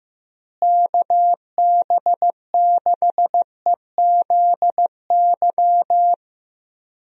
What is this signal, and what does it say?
Morse code, heard as a keyed tone, about 16 wpm, 700 Hz. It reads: KB6EZY
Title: USA Call Signs - 6 Characters - Repeated 3x - 15wpm